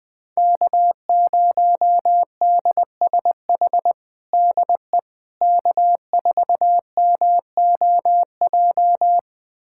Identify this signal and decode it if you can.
Morse code, heard as a keyed tone, about 20 wpm, 700 Hz. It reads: K0DSH DE K4MOJ
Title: USA Callsign DE Callsign - 5 Characters - 20wpm